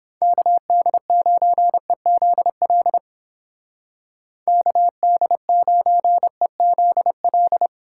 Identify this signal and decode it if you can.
Morse code, heard as a keyed tone, about 31 wpm, 700 Hz. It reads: KD9EZL KD9EZL